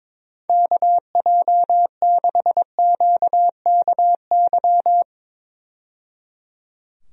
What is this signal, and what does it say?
Morse code, heard as a keyed tone, about 22 wpm, 700 Hz. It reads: KJ6QKY
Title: USA Call Signs - 6 Characters - 22wpm